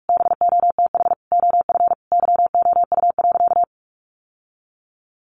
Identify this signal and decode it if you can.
Morse code, heard as a keyed tone, about 44 wpm, 700 Hz. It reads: BOTH OF YOU.
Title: Sentences from Top 1200 Words 45wpm